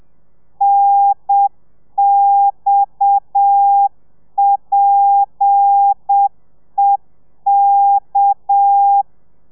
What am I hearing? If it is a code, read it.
Morse code, heard as a keyed tone, about 7 wpm, 800 Hz. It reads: NXPEK
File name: telegrafia / i2tfj / Lez7 30